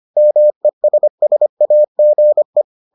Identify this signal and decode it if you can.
Morse code, heard as a keyed tone, about 25 wpm, 600 Hz. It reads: MESSAGE